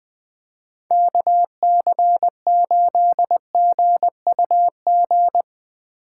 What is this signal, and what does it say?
Morse code, heard as a keyed tone, about 20 wpm, 700 Hz. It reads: KC8GUG
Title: USA Call Signs - 3x Character Spacing - 20wpm